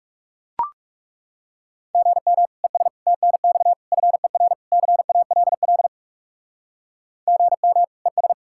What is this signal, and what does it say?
Morse code, heard as a keyed tone, about 37 wpm, 700 Hz. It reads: GM ES TNX FER CALL GM ES
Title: Speed-Racing Common QSO Phrases - 30wpm